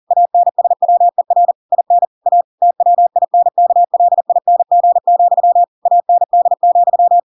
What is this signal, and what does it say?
Morse code, heard as a keyed tone, about 40 wpm, 700 Hz. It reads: ANSWER IN A TWINKLING, AND,
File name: Pinocchio / Pinocchio ch8 40 40 ik1hge